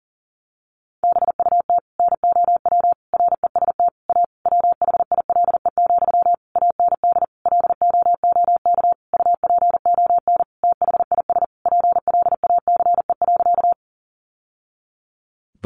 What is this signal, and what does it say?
Morse code, heard as a keyed tone, about 40 wpm, 700 Hz. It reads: BUT NOW REST A WHILE, AND LOOK UPON THIS PLACE.
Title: Sentences from Top 1100 Words 40wpm